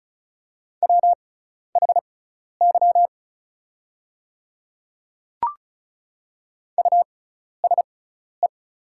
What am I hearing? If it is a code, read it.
Morse code, heard as a keyed tone, about 35 wpm, spaced as for 12 wpm, 690 Hz. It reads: WHY USE
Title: 6x Character Spacing - Top 300 Words - 35wpm